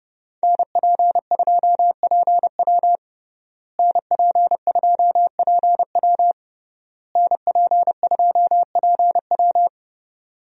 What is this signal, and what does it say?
Morse code, heard as a keyed tone, about 30 wpm, 700 Hz. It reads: NP2PW NP2PW NP2PW
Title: USA Call Signs - 5 Characters - Repeated 3x - 30wpm